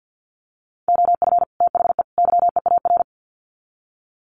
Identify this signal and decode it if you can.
Morse code, heard as a keyed tone, about 50 wpm, 700 Hz. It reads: OF THE YEAR
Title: Sets of 3 Words 50wpm